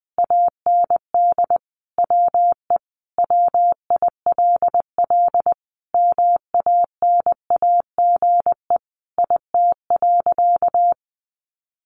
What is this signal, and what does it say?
Morse code, heard as a keyed tone, about 20 wpm, 700 Hz. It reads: AND WE WILL MANAGE IT.
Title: Sentences from Top 1400 Words 20wpm